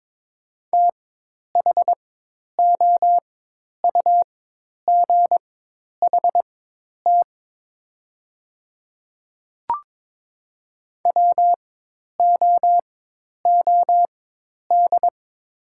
Morse code, heard as a keyed tone, about 22 wpm, spaced as for 10 wpm, 700 Hz. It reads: THOUGHT WOOD
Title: 4x Character Spacing - Top 300 Words - 22wpm